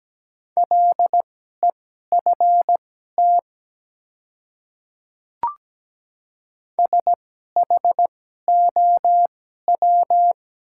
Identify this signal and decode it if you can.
Morse code, heard as a keyed tone, about 17 wpm, spaced as for 12 wpm, 700 Hz. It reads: LEFT SHOW
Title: 2x Character Spacing - Top 300 Words - 17wpm